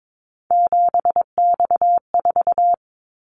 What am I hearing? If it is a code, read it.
Morse code, heard as a keyed tone, about 22 wpm, 700 Hz. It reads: 7X4
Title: Callsign Prefixes for Top 100 DXCC - 22wpm